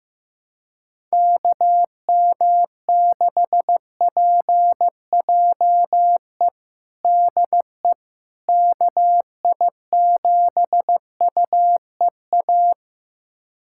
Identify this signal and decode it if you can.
Morse code, heard as a keyed tone, about 15 wpm, 700 Hz. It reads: KM6PJE DE KI7UEA
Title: USA Callsign DE Callsign 15wpm